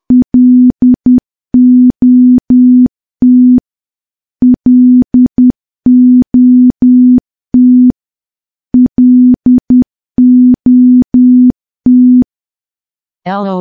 Morse code, heard as a keyed tone, about 10 wpm, 250 Hz. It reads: LOT LOT LOT
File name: morse / morse-training-10WPM